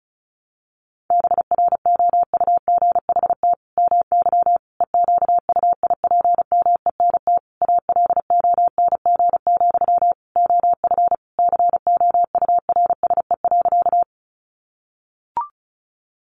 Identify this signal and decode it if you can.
Morse code, heard as a keyed tone, about 34 wpm, 700 Hz. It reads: BROUGHT MY EQUIPMENT ALONG, OF COURSE.
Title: Sentences from Top 700 Words 35wpm